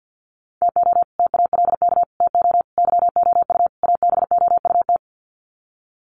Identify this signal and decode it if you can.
Morse code, heard as a keyed tone, about 50 wpm, 700 Hz. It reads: TO TALK TO YOU ABOUT
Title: Sets of 5 Words Encore 50wpm